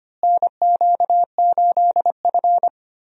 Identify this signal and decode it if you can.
Morse code, heard as a keyed tone, about 25 wpm, 700 Hz. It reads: NQ8F
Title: USA Call Signs - 4 Characters - Repeated 3x - 25wpm